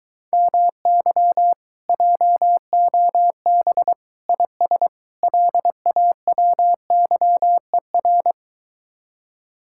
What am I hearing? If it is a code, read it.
Morse code, heard as a keyed tone, about 23 wpm, 700 Hz. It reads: MY JOB IS LAWYER